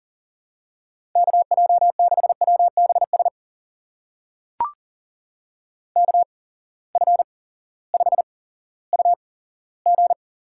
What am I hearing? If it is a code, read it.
Morse code, heard as a keyed tone, about 40 wpm, spaced as for 18 wpm, 700 Hz. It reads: KJ6WBS KF5UG
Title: USA Call Signs - 6 Characters - 8x Character Spacing - 40wpm